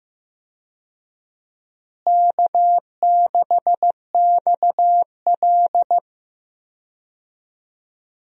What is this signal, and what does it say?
Morse code, heard as a keyed tone, about 15 wpm, 700 Hz. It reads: K6XL
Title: USA Call Signs - 4 Characters - Repeated 2x - 15wpm